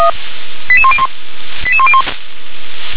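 Five-tone selective call (ZVEI), heard as quick runs of five tones, tones of about 70 ms. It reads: 88101
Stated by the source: ZVEI five-tone selective call received by SDR.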